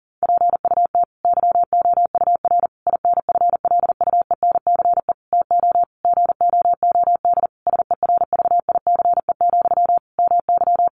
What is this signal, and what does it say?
Morse code, heard as a keyed tone, about 40 wpm, 700 Hz. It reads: PUT YOUR INFLUENCE TO GOOD SERVICE, MY
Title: Sentences from Top 1100 Words 40wpm